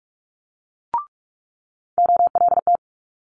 Morse code, heard as a keyed tone, about 46 wpm, 700 Hz. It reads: OLT
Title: Multi-letter Phonemes Quick Fire 45wpm